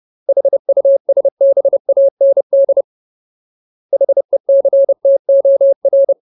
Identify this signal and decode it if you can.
Morse code, heard as a keyed tone, about 30 wpm, 550 Hz. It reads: HUSBAND HECTOR
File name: Ultimate Word List / 30WPM Ultimate